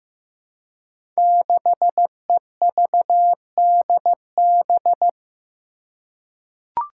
Morse code, heard as a keyed tone, about 15 wpm, 700 Hz. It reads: 6EVDB
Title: Five Challenging Letter-Number Combinations 15wpm